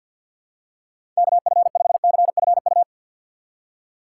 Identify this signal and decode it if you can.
Morse code, heard as a keyed tone, about 47 wpm, 700 Hz. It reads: KF5CFU
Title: USA Call Signs - 6 Characters - 3x Character Spacing - 50wpm